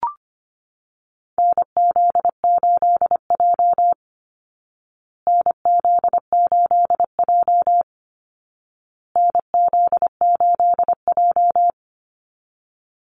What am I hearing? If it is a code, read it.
Morse code, heard as a keyed tone, about 25 wpm, 700 Hz. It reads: NZ8J NZ8J NZ8J
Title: Call Signs Repeated 2x - 25wpm